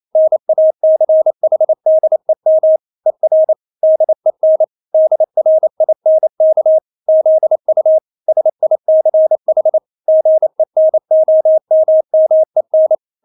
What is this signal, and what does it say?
Morse code, heard as a keyed tone, about 28 wpm, 630 Hz. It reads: NACHDEM ER DEN DRINK ZU SICH GENOMMEN
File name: Witz9 28wpm